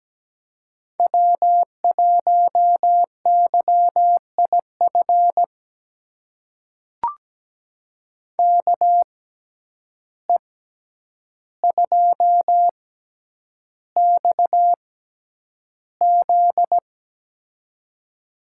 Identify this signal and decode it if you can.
Morse code, heard as a keyed tone, about 17 wpm, spaced as for 9 wpm, 700 Hz. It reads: W1YIF KE2XZ
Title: USA Call Signs - 6x Character Spacing - 17wpm